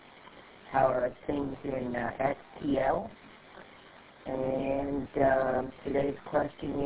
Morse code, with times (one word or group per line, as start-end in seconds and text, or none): none